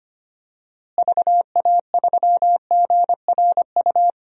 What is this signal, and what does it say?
Morse code, heard as a keyed tone, about 25 wpm, 700 Hz. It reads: VA3GRU
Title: DXCC-100 Call Signs - 6 Characters - Repeated 2x - 25wpm